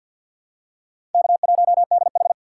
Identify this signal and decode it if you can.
Morse code, heard as a keyed tone, about 50 wpm, 700 Hz. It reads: K1DH
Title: USA Call Signs - 4 Characters - 2x Character Spacing - 50wpm